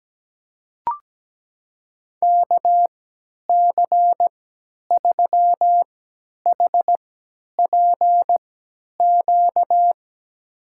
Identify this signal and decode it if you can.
Morse code, heard as a keyed tone, about 17 wpm, spaced as for 10 wpm, 700 Hz. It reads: KC3HPQ
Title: USA Call Signs - 6 Characters - 3x Character Spacing - 17wpm